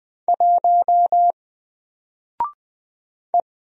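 Morse code, heard as a keyed tone, about 20 wpm, 700 Hz. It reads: 1 E
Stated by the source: Single letter and number Morse drill at 20 wpm, from N4LCD.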